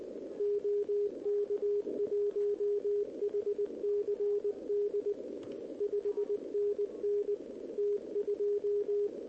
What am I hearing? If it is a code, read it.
Morse code, heard as a keyed tone, about 19 wpm, 410 Hz. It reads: OK1HCD 5NN T2
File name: OK QRP 2026 / OKQRP2026